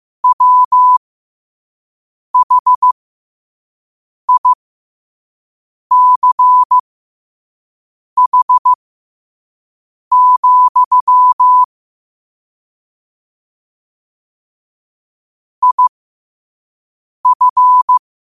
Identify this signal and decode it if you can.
Morse code, heard as a keyed tone, about 15 wpm, spaced as for 5 wpm, 1000 Hz. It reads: WHICH, IF
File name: qotdfiles / 5wpm